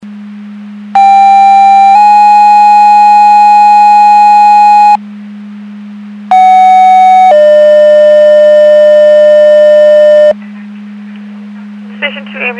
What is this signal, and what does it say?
Two-tone page, heard as a one-second tone, then a three-second tone, about 790 then 830 Hz, about 750 then 600 Hz. It is heard from 1.0 to 10.3 s.
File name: Portage County 154.3850 Fire EMS Paging